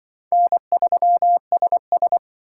Morse code, heard as a keyed tone, about 23 wpm, 700 Hz. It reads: N3SS
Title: Speed Racing - USA 1x2 Call Signs 20wpm